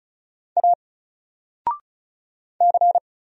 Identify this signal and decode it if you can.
Morse code, heard as a keyed tone, about 34 wpm, 700 Hz. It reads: A C
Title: Single Character (Letter:Number:Punctuation:Prosign) 35wpm